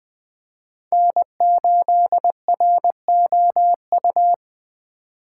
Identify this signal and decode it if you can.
Morse code, heard as a keyed tone, about 23 wpm, 700 Hz. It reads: N8ROU
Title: Speed Racing - USA 1x3 Call Signs 20wpm